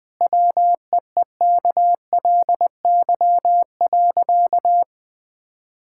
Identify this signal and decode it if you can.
Morse code, heard as a keyed tone, about 20 wpm, 700 Hz. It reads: WEEKLY.
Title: Difficult Sentences with Hints - 20wpm